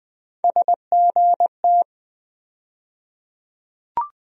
Morse code, heard as a keyed tone, about 20 wpm, 700 Hz. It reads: SGT
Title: USA Call Sign Postfixes - 3 Letters 20wpm